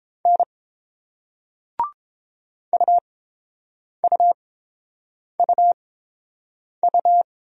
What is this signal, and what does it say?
Morse code, heard as a keyed tone, about 30 wpm, spaced as for 7 wpm, 690 Hz. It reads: N UUUU